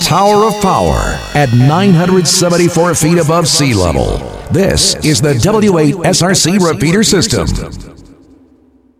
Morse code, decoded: none